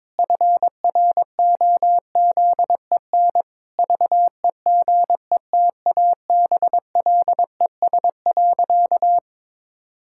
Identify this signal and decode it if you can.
Morse code, heard as a keyed tone, about 22 wpm, 700 Hz. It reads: FROZEN VEGETABLES.
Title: Pangram Challenge v2 - 22wpm